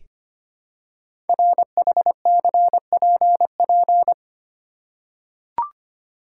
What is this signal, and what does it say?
Morse code, heard as a keyed tone, about 25 wpm, 700 Hz. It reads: RHCPP
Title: Code Groups - 5 Letters - 25wpm